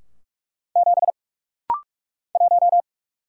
Morse code, heard as a keyed tone, about 44 wpm, 700 Hz. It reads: 7 1
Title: Single Letter-Number - Rapid-Fire 45wpm